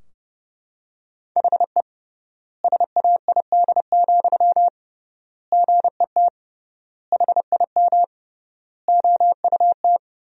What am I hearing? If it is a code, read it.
Morse code, heard as a keyed tone, about 30 wpm, 700 Hz. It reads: HE SAID, GET HIM OUT
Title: 3x Word-Spacing - Sentences from Top 400 Words - 30wpm